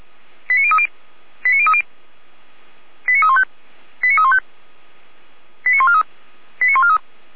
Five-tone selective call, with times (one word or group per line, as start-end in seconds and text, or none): 0.5-0.9 89030
1.5-1.8 89030
3.1-3.4 89315
4.0-4.4 89315
5.7-6.0 89143
6.6-7.0 89143